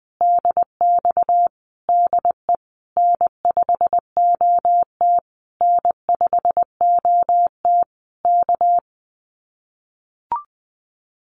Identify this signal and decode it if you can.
Morse code, heard as a keyed tone, about 20 wpm, 700 Hz. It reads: DX DE N5OT N5OT K
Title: Common QSO Phrases Encore 20wpm